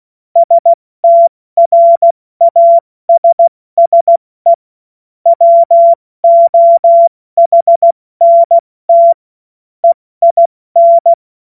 Morse code, heard as a keyed tone, about 16 wpm, spaced as for 14 wpm, 680 Hz. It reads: STRASSE WOHNT EIN
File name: Morsekurs / Witze / Witz 13. September 2022 70 F80